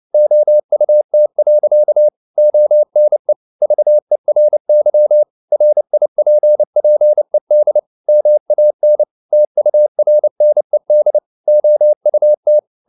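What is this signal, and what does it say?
Morse code, heard as a keyed tone, about 29 wpm, 600 Hz. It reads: OUT. ONE VERY RIPPED MAN TURNED OUT
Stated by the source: Morse code at 29 wpm, sending Ken Follett's novel Never in English, from CW med Gnister.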